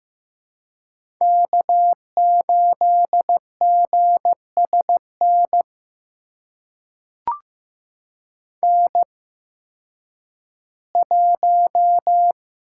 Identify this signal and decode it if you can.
Morse code, heard as a keyed tone, about 15 wpm, spaced as for 10 wpm, 700 Hz. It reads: K8GSN N1
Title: USA Call Signs - 5 Characters - 8x Character Spacing - 15wpm